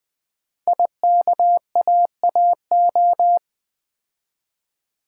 Morse code, heard as a keyed tone, about 20 wpm, 700 Hz. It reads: IKAAO
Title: Code Groups - 5 Letters - 20wpm